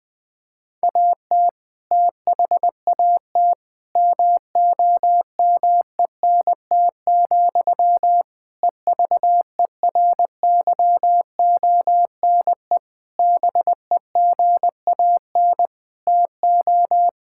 Morse code, heard as a keyed tone, about 21 wpm, 700 Hz. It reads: AT THAT MOMENT, EVERYONE BEGAN TO
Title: Sentences from Top 1000 Words - Review Part 5 - 20wpm